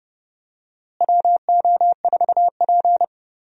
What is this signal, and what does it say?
Morse code, heard as a keyed tone, about 31 wpm, 700 Hz. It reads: WO4P
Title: USA Call Signs - 4 Characters - Repeated 3x - 30wpm